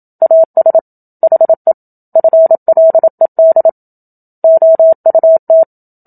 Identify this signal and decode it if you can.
Morse code, heard as a keyed tone, about 27 wpm, 650 Hz. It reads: AS HE FLED OUT
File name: Book Two / 01-Chapter 1 - 27 WPM